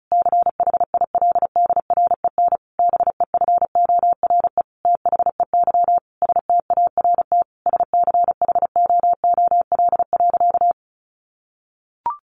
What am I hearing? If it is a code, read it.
Morse code, heard as a keyed tone, about 34 wpm, 700 Hz. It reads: CHILDREN BEFORE THEY START SCHOOL.
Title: Sentences from Top 1100 Words 35wpm